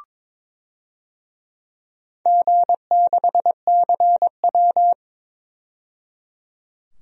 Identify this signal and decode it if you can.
Morse code, heard as a keyed tone, about 22 wpm, 700 Hz. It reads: G6CW
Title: DXCC-100 Call Signs - 4 Characters - 22wpm